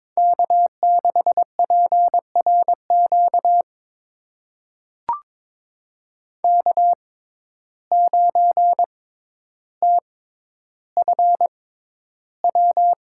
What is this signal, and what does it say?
Morse code, heard as a keyed tone, about 22 wpm, spaced as for 11 wpm, 700 Hz. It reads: K6PRQ K9TFW
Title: USA Call Signs - 5 Characters - 6x Character Spacing - 22wpm